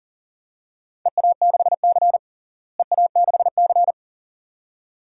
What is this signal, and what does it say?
Morse code, heard as a keyed tone, about 40 wpm, 700 Hz. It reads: EA6C EA6C
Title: DXCC-100 Call Signs - 4 Characters - Repeated 2x - 40wpm